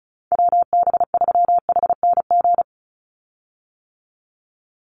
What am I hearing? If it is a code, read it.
Morse code, heard as a keyed tone, about 34 wpm, 700 Hz. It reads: WB3HNG
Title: USA Call Signs 35wpm